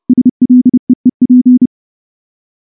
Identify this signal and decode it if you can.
Morse code, heard as a keyed tone, about 28 wpm, 250 Hz. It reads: SLEEP
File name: morse / morse-training-30WPM